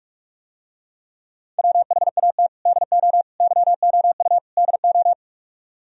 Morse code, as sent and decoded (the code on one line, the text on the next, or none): .-- .... .- - / -.. --- / -.-- --- ..- / -.. ---
WHAT DO YOU DO